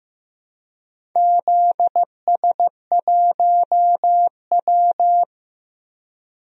Morse code, heard as a keyed tone, about 15 wpm, 700 Hz. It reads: ZS1W
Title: DXCC-100 Call Signs - 4 Characters - Repeated 2x - 15wpm